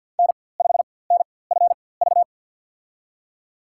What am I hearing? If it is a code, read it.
Morse code, heard as a keyed tone, about 49 wpm, spaced as for 23 wpm, 700 Hz. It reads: N5NFV
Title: USA Call Signs - 5 Characters - 4x Character Spacing - 50wpm